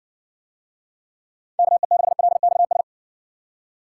Morse code, heard as a keyed tone, about 54 wpm, 700 Hz. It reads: KE6LXS